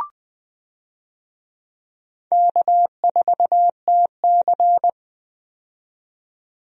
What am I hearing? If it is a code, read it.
Morse code, heard as a keyed tone, about 20 wpm, 700 Hz. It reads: K4TC
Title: USA Call Signs - 4 Characters - 20wpm